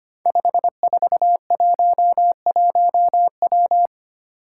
Morse code, heard as a keyed tone, about 25 wpm, 700 Hz. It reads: 5411W
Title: Five Challenging Letter-Number Combinations 25wpm